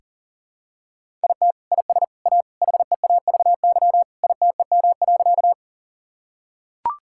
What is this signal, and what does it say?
Morse code, heard as a keyed tone, about 40 wpm, 700 Hz. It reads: IT IS A HEAVY ITEM.
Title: Sentences from Top 1000 Words - Review Part 6 - 40wpm